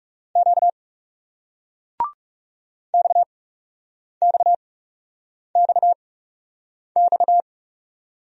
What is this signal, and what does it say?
Morse code, heard as a keyed tone, about 39 wpm, spaced as for 8 wpm, 700 Hz. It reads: Q XXXX